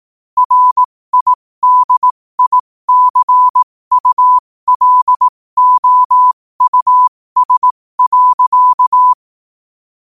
Morse code, heard as a keyed tone, about 18 wpm, spaced as for 15 wpm, 1000 Hz. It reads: RIDICULOUS.